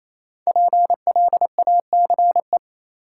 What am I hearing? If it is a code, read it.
Morse code, heard as a keyed tone, about 28 wpm, 700 Hz. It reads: PLACE
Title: Sets of 4 Words - New Word Drill - 28wpm